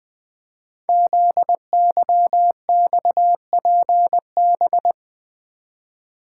Morse code, heard as a keyed tone, about 21 wpm, 700 Hz. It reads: ZYXPB
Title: Code Groups - 5 Letter:Numbers - 20wpm